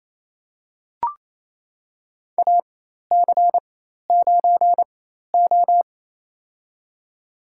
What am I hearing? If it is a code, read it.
Morse code, heard as a keyed tone, about 28 wpm, spaced as for 13 wpm, 700 Hz. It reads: AC9O
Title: USA Call Signs - 4 Characters - 4x Character Spacing - 28wpm